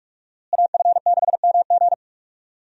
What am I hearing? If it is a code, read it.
Morse code, heard as a keyed tone, about 45 wpm, 700 Hz. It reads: AF6MG
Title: USA Call Signs - 5 Characters - Repeated 3x - 45wpm